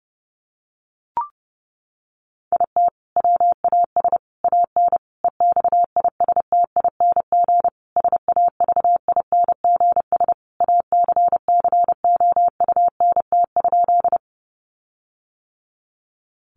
Morse code, heard as a keyed tone, about 30 wpm, 700 Hz. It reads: IT WAS AN EXISTING SAVINGS ACCOUNT?
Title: Sentences from Top 1500 Words 30wpm